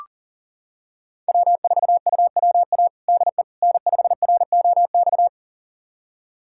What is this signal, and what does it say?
Morse code, heard as a keyed tone, about 37 wpm, 700 Hz. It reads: W4UWA DE N5ROX
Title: USA Callsign DE Callsign - 5 Characters - 40wpm